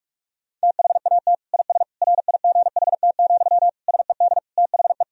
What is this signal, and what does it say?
Morse code, heard as a keyed tone, about 42 wpm, 700 Hz. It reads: THAT IS RIGHT, SED THE
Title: Sentences from Top 200 Words and Abbreviations 45wpm